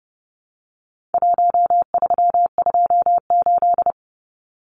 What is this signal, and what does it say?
Morse code, heard as a keyed tone, about 30 wpm, 700 Hz. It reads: 1328
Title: Numbers - Contest Style - 4-Digit 30wpm